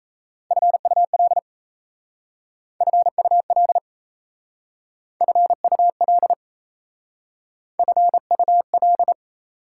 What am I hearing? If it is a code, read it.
Morse code, heard as a keyed tone, about 33 wpm, 700 Hz. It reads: FUL FUL FUL FUL
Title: Speed-Racing Prefixes and Suffixes 28wpm